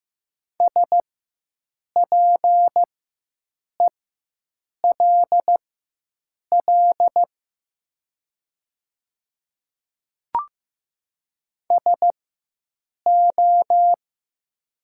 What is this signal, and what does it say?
Morse code, heard as a keyed tone, about 15 wpm, spaced as for 7 wpm, 700 Hz. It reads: SPELL SO
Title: 4x Character Spacing - Top 200 Words - 15wpm